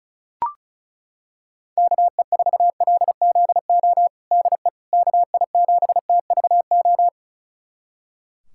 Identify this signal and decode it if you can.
Morse code, heard as a keyed tone, about 36 wpm, 700 Hz. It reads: KE4LZO DE KI7TVO
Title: USA Callsign DE Callsign - 6 Characters - 35wpm